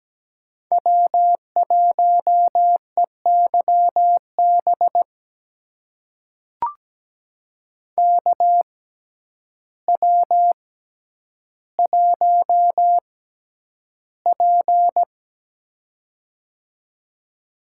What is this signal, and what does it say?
Morse code, heard as a keyed tone, about 17 wpm, spaced as for 9 wpm, 700 Hz. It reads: W1EYB KW1P
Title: USA Call Signs - 6x Character Spacing - 17wpm